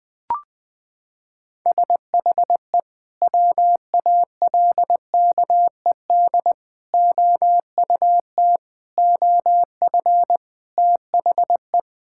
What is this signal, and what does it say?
Morse code, heard as a keyed tone, about 20 wpm, 700 Hz. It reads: SHE WALKED OUT OF THE
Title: Sentences with 3-Word Phrases - 20wpm